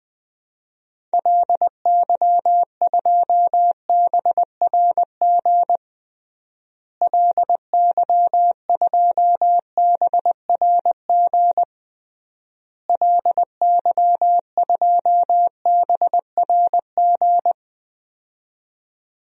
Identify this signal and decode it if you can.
Morse code, heard as a keyed tone, about 20 wpm, 700 Hz. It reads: LY2BRG LY2BRG LY2BRG
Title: DXCC-100 Call Signs - 6 Characters - Repeated 3x - 20wpm